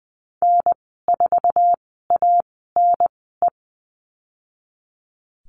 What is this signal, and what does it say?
Morse code, heard as a keyed tone, about 20 wpm, spaced as for 14 wpm, 700 Hz. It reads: N4ANE
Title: 069 Lesson 12g - Letter d - Call Signs 20wpm